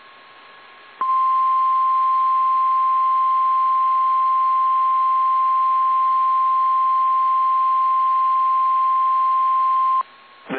Emergency Alert System (SAME) tones (the 1050 Hz attention tone). It is heard from 0.9 to 10.1 s.
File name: eas / KSDZKDJL / KSDZKDJL